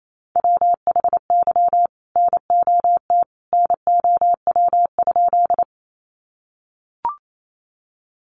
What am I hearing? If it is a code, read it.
Morse code, heard as a keyed tone, about 28 wpm, 700 Hz. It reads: WHY NOT NOW?
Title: Sentences from Top 600 Words 28wpm